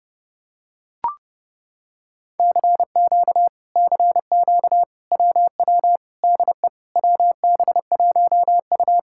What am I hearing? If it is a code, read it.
Morse code, heard as a keyed tone, about 28 wpm, 700 Hz. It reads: CQ CQ WW DE WB1U